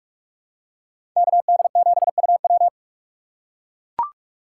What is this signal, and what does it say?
Morse code, heard as a keyed tone, about 42 wpm, 700 Hz. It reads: KD7UW
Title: USA Call Signs - 5 Characters - 3x Character Spacing - 45wpm